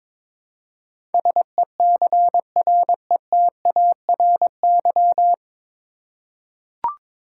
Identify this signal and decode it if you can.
Morse code, heard as a keyed tone, about 23 wpm, 700 Hz. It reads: SECRETARY
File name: 9 Letter Words - First 3 Letters Given - 22wpm